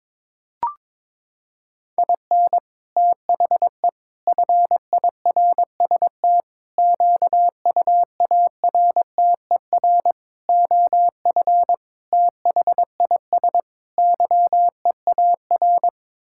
Morse code, heard as a keyed tone, about 22 wpm, 700 Hz. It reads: IN THE FIRST QUARTER OF THIS YEAR
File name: practice / 22wpm / Sets of 7 Words - 22wpm